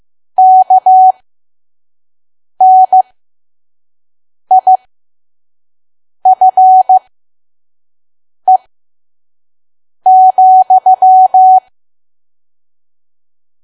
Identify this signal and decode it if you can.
Morse code, heard as a keyed tone, about 15 wpm, spaced as for 5 wpm, 750 Hz. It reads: KNIFE,
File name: Archive / 5 WPM files / PM